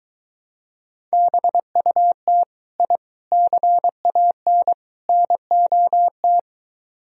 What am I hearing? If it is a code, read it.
Morse code, heard as a keyed tone, about 22 wpm, 700 Hz. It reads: BUT I CAN NOT